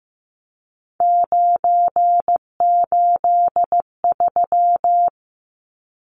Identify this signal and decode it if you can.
Morse code, heard as a keyed tone, about 15 wpm, 700 Hz. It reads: 983
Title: Numbers - 3-Digit 15wpm